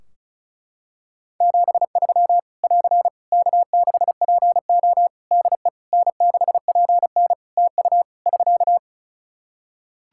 Morse code, heard as a keyed tone, about 36 wpm, 700 Hz. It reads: 73 <AR> K6PO DE N6PN TU <SK>
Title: K1USN SST QSO Practice 35wpm